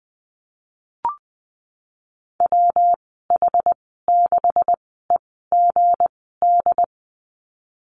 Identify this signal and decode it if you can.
Morse code, heard as a keyed tone, about 20 wpm, spaced as for 14 wpm, 700 Hz. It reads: WH6EGD
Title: 161 Lesson 26g - Letter g - Call Signs 20wpm